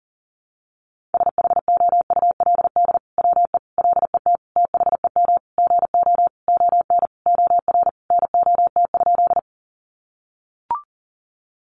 Morse code, heard as a keyed tone, about 40 wpm, 700 Hz. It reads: SHOULD WE LET THEM GO ON OR NOT?
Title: Sentences from Top 600 Words 40wpm